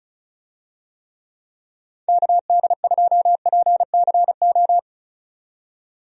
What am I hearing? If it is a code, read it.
Morse code, heard as a keyed tone, about 36 wpm, 700 Hz. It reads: KD2PCO
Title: USA Call Signs - 6 Characters - 35wpm